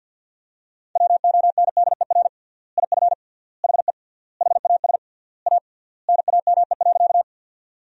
Morse code, heard as a keyed tone, about 50 wpm, 700 Hz. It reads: WONDER IF HE HAS A NAME.
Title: 3x Word-Spacing - Sentences from Top 500 Words - 50wpm